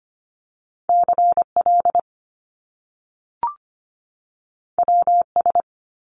Morse code, heard as a keyed tone, about 25 wpm, 700 Hz. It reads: CL WS
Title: Top Two Letter Combinations in Words 25wpm